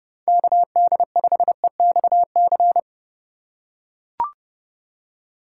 Morse code, heard as a keyed tone, about 30 wpm, 700 Hz. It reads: KD5EXC
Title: USA Call Signs - 6 Characters - 3x Character Spacing - 30wpm